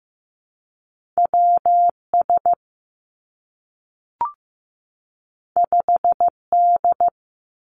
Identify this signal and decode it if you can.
Morse code, heard as a keyed tone, about 15 wpm, 700 Hz. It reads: W S 5 D
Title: Two Challenging Letter-Number Combinations 15wpm